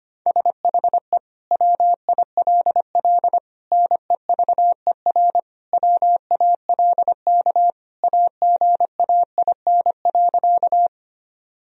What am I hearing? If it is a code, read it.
Morse code, heard as a keyed tone, about 25 wpm, 700 Hz. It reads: SHE WILL NEVER WALK AGAIN.
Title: Sentences from Top 1000 Words - Review Part 1 - 25wpm